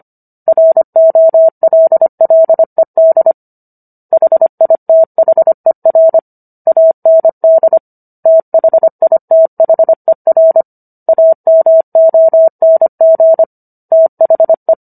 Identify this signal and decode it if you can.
Morse code, heard as a keyed tone, about 25 wpm, 650 Hz. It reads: ROLLED HITHER AND THITHER AMONG THE